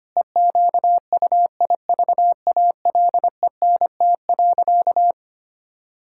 Morse code, heard as a keyed tone, about 25 wpm, 700 Hz. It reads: EQUIVALENT.
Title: Sentences from Top 1600 Words 25wpm